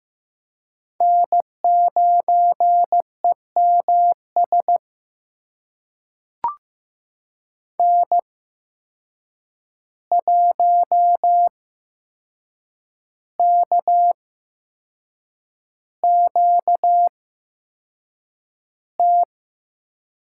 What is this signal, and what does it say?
Morse code, heard as a keyed tone, about 15 wpm, spaced as for 6 wpm, 700 Hz. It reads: N9EMS N1KQT
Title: USA Call Signs - 5 Characters - 8x Character Spacing - 15wpm